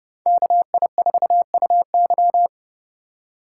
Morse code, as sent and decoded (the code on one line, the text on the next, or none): -.- .. ....- ..- -.--
KI4UY